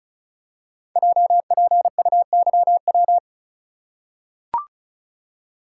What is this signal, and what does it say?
Morse code, heard as a keyed tone, about 36 wpm, 700 Hz. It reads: JPUYW
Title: Code Groups - 5 Letters - 35wpm